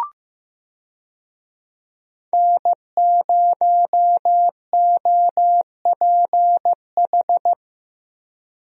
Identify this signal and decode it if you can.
Morse code, heard as a keyed tone, about 16 wpm, 700 Hz. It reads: N0OPH
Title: USA Call Signs - 5 Characters - Repeated 3x - 15wpm